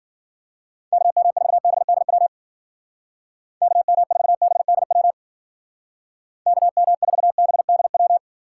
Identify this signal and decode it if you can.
Morse code, heard as a keyed tone, about 49 wpm, 700 Hz. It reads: KM4BDW KM4BDW KM4BDW
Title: Speed Racing - USA Call Signs 40wpm